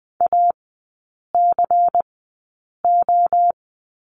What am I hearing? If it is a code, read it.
Morse code, heard as a keyed tone, about 20 wpm, spaced as for 8 wpm, 700 Hz. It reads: ACO
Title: 127 Lesson 21e - Number 6 - Three Characters 20wpm